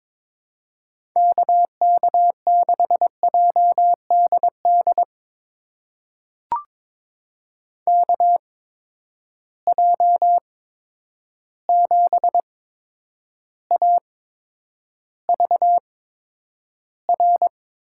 Morse code, heard as a keyed tone, about 22 wpm, spaced as for 9 wpm, 700 Hz. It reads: KK6JDD KJ7AVR
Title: USA Call Signs - 6 Characters - 8x Character Spacing - 22wpm